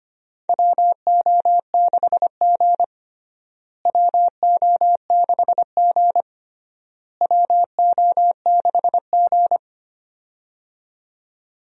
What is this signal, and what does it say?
Morse code, heard as a keyed tone, about 25 wpm, 700 Hz. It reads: WO6G WO6G WO6G
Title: USA Call Signs - 4 Characters - Repeated 3x - 25wpm